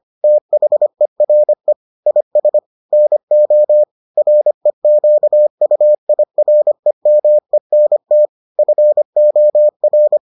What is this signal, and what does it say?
Morse code, heard as a keyed tone, about 25 wpm, 600 Hz. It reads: THERE IS NO REQUIREMENT FOR